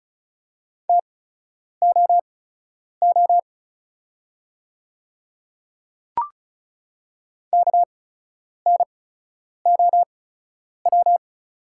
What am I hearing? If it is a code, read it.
Morse code, heard as a keyed tone, about 36 wpm, spaced as for 10 wpm, 700 Hz. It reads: TOO KNOW
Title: 8x Character Spacing - Top 200 Words - 35wpm